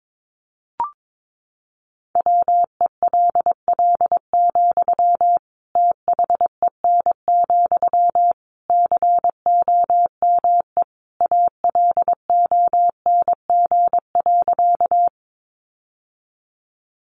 Morse code, heard as a keyed tone, about 22 wpm, 700 Hz. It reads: WELL, THEN, COME ALONG.
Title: Sentences from Top 700 Words 22wpm